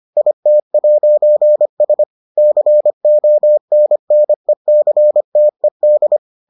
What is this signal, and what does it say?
Morse code, heard as a keyed tone, about 25 wpm, 600 Hz. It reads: IT'S CONNECTED